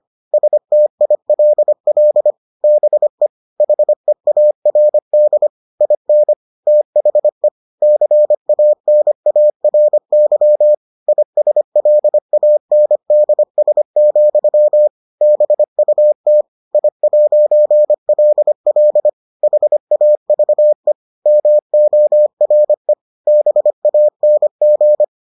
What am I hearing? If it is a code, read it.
Morse code, heard as a keyed tone, about 25 wpm, 600 Hz. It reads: STILL BE HEARD IN THE CANARY ISLANDS, BUT I'LL HAVE MORE BANG